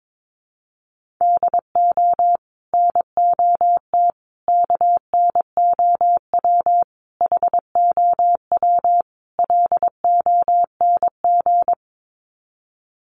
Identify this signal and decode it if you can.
Morse code, heard as a keyed tone, about 22 wpm, 700 Hz. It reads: DO NOT KNOW HOW LONG
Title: Sets of 5 Words Encore2 22wpm